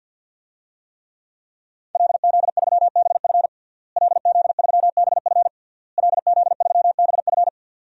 Morse code, heard as a keyed tone, about 50 wpm, 700 Hz. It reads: LZ3BF LZ3BF LZ3BF
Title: DXCC-100 Call Signs - 5 Characters - Repeated 3x - 50wpm